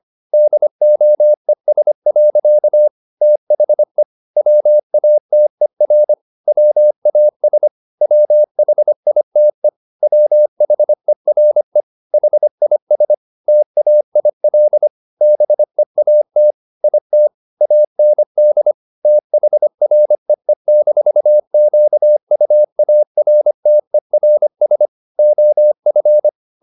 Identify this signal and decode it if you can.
Morse code, heard as a keyed tone, about 25 wpm, 600 Hz. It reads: DOES. THE WATER WAS WHITE WHERE HIS TAIL BEAT IT AND THREE-QUARTERS OF